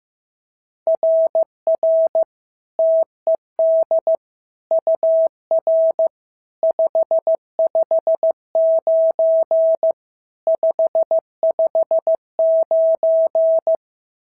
Morse code, heard as a keyed tone, about 14 wpm, 650 Hz. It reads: RR TED UR 559 559
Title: SOTA QSO Practice 15wpm